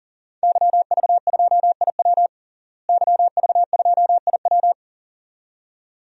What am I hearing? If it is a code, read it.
Morse code, heard as a keyed tone, about 40 wpm, 700 Hz. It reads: YV2IW YV2IW
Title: DXCC-100 Call Signs - 5 Characters - Repeated 2x - 40wpm